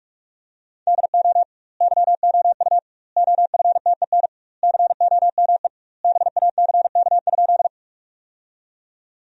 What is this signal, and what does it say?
Morse code, heard as a keyed tone, about 42 wpm, 700 Hz. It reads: DO YOU OFTEN COME BACK?
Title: 2x Word-Spacing - Sentences from Top 300 Words - 45wpm